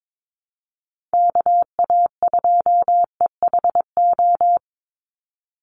Morse code, heard as a keyed tone, about 22 wpm, 700 Hz. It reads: KA2EHO
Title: USA Call Signs 22wpm